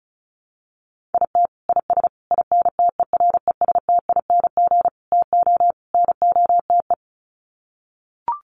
Morse code, heard as a keyed tone, about 34 wpm, 700 Hz. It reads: IT IS INTERESTING TO NOTE
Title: Sets of 5 Words Encore2 35wpm